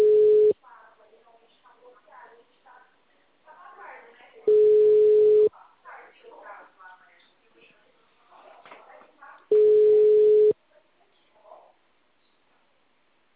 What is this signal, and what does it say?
Ringback tone (Europe, 425 Hz), with speech.
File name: out- -2015- - - .29449